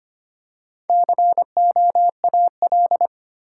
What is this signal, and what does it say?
Morse code, heard as a keyed tone, about 25 wpm, 700 Hz. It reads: COAL
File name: practice / 25wpm / Easy Words with Hints - 25wpm